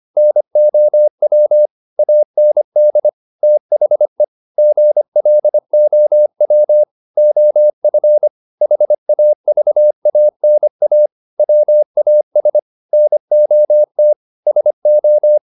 Morse code, heard as a keyed tone, about 25 wpm, 600 Hz. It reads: NOW AND THE GLOW OF HAVANA WAS NOT SO
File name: the old man and the sea part 1 25wpm